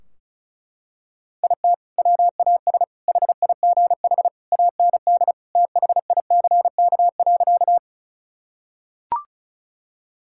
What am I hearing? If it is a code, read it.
Morse code, heard as a keyed tone, about 36 wpm, 700 Hz. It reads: IT WAS HIGH AND THICK.
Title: Sentences from Top 1600 Words 35wpm